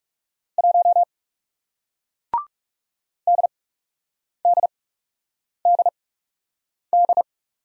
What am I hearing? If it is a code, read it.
Morse code, heard as a keyed tone, about 37 wpm, spaced as for 8 wpm, 700 Hz. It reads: 1 DDDD
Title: Speed-Racing Single Letter-Number 30wpm